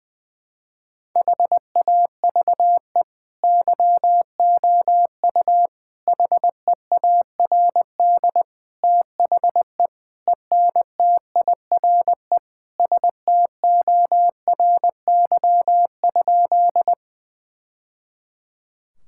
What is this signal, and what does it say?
Morse code, heard as a keyed tone, about 20 wpm, 700 Hz. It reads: HAVE YOU HEARD THE ENTIRE STORY?
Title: Sentences from Top 1000 Words - Review Part 5 - 20wpm